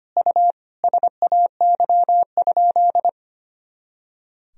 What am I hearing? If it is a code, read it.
Morse code, heard as a keyed tone, about 25 wpm, 700 Hz. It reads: U SAY?
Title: Sentences from Top 400 Words and Abbreviations 25wpm